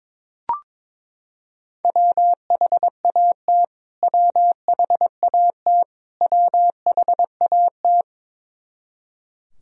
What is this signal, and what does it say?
Morse code, heard as a keyed tone, about 22 wpm, 700 Hz. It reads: WHAT WHAT WHAT
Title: Top 100 Words Repeated 22wpm